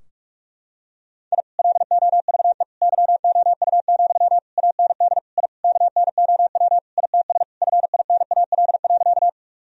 Morse code, heard as a keyed tone, about 47 wpm, 700 Hz. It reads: I LOVE YOU, AND I KNOW ITS FINAL.
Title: Sentences from Top 1000 Words - Review Part 2 - 45wpm